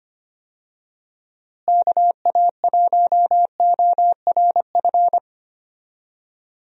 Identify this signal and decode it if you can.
Morse code, heard as a keyed tone, about 25 wpm, 700 Hz. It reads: KA1ORF
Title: USA Call Signs - 6 Characters - Repeated 3x - 25wpm